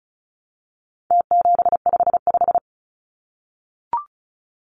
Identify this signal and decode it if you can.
Morse code, heard as a keyed tone, about 34 wpm, 700 Hz. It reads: T755
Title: Numbers - Contest Style - 4-Digit 35wpm